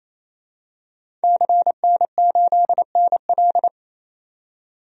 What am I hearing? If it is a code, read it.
Morse code, heard as a keyed tone, about 28 wpm, 700 Hz. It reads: CN8NL
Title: DXCC-100 Call Signs - 5 Characters - 28wpm